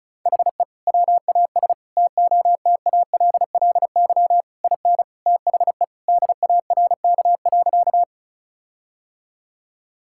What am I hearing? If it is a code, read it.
Morse code, heard as a keyed tone, about 36 wpm, 700 Hz. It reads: HE WAS TOTALLY IN THE DARK.
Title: Sentences from Top 1600 Words 35wpm